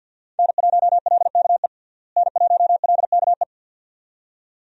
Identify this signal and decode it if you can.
Morse code, heard as a keyed tone, about 49 wpm, 700 Hz. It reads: N1LKE N1LKE
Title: USA Call Signs - 5 Characters - Repeated 2x - 50wpm